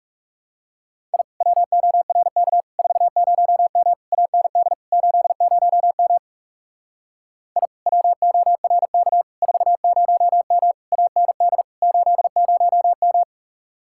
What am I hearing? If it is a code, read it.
Morse code, heard as a keyed tone, about 38 wpm, 700 Hz. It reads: I WORK 40M AND 80M I WORK 40M AND 80M